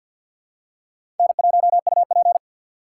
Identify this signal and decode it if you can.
Morse code, heard as a keyed tone, about 51 wpm, 700 Hz. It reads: N1UP
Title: USA Call Signs - 3x Character Spacing - 50wpm